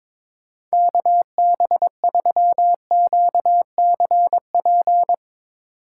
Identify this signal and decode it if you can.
Morse code, heard as a keyed tone, about 22 wpm, 700 Hz. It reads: KB3QCP
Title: Speed Racing - USA 2x3 Call Signs 22wpm